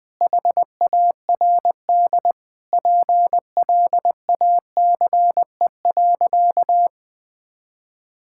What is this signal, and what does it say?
Morse code, heard as a keyed tone, about 20 wpm, 700 Hz. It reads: HARD PLACE.
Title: Sentences with 8-Word Phrases - Spoken Beginning - 20wpm